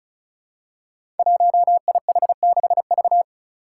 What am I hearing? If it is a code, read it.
Morse code, heard as a keyed tone, about 36 wpm, 700 Hz. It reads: 1IH6V
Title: Five Challenging Letter-Number Combinations 35wpm